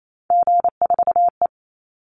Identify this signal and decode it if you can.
Morse code, heard as a keyed tone, about 28 wpm, 700 Hz. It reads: G4E
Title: Call Signs - 28wpm